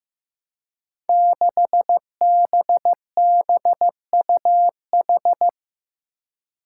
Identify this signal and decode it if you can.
Morse code, heard as a keyed tone, about 15 wpm, 700 Hz. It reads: 6BBUH
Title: Five Challenging Letter-Number Combinations 15wpm